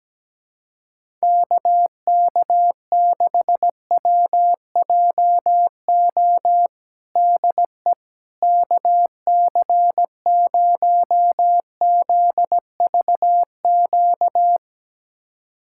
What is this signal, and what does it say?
Morse code, heard as a keyed tone, about 17 wpm, 700 Hz. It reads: KK6WJO DE KC0ZVQ
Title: USA Callsign DE Callsign 17wpm